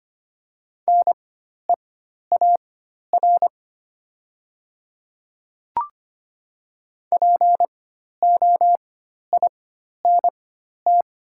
Morse code, heard as a keyed tone, about 25 wpm, spaced as for 12 wpm, 700 Hz. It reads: NEAR POINT
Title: 4x Character Spacing - Top 300 Words - 25wpm